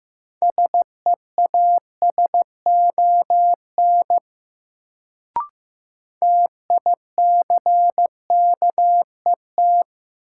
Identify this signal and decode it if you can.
Morse code, heard as a keyed tone, about 15 wpm, 700 Hz. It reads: SEASON TICKET
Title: Sentences from Top 1400 Words - New Word Drill - 15wpm